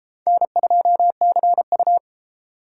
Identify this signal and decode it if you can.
Morse code, heard as a keyed tone, about 29 wpm, 700 Hz. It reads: N2CU